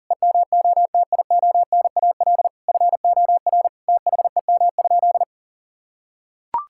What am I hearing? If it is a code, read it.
Morse code, heard as a keyed tone, about 40 wpm, 700 Hz. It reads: EMOTIONAL FOR THEM?
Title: Sentences from Top 1000 Words - Review Part 5 - 40wpm